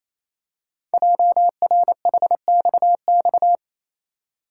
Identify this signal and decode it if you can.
Morse code, heard as a keyed tone, about 29 wpm, 700 Hz. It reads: JRHXX
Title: Code Groups - 5 Letters - 28wpm